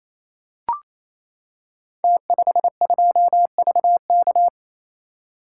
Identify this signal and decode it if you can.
Morse code, heard as a keyed tone, about 27 wpm, 700 Hz. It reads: T52VK
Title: Code Groups - 5 Letter:Numbers - 28wpm